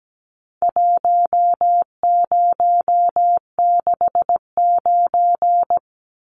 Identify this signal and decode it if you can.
Morse code, heard as a keyed tone, about 17 wpm, 700 Hz. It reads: 1069
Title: Numbers - Contest Style - 4-Digit 17wpm